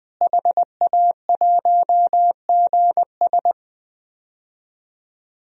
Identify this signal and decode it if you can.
Morse code, heard as a keyed tone, about 20 wpm, 700 Hz. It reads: HA1GS
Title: DXCC-100 Call Signs - 5 Characters - Repeated 2x - 20wpm